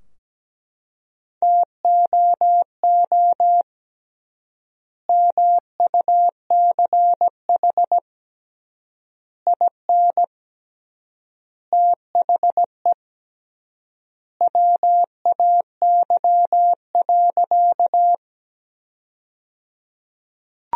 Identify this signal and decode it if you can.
Morse code, heard as a keyed tone, about 17 wpm, 700 Hz. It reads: TOO MUCH IN THE WAY.
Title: 3x Word-Spacing - Sentences from Top 200 Words - 17wpm